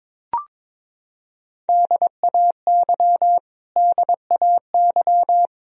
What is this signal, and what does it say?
Morse code, heard as a keyed tone, about 22 wpm, 700 Hz. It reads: DAY DAY
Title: Top 100 Words Repeated 22wpm